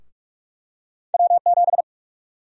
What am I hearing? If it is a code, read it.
Morse code, heard as a keyed tone, about 46 wpm, 700 Hz. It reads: W7
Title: USA Call Sign Prefixes - 45wpm